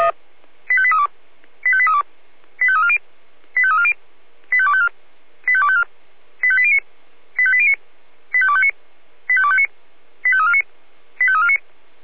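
Five-tone selective call (ZVEI), heard as quick runs of five tones, tones of about 70 ms. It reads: 85832 85832 85340 85340 85254 85254 85809 85809 85259 85259 85349 85349
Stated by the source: ZVEI five-tone selective call received on SDR.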